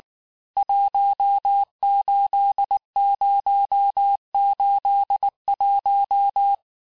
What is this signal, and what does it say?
Morse code, heard as a keyed tone, about 19 wpm, 800 Hz. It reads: 18081
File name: cours-du-lundi files / cours-1583-fin-QSO-18-mpm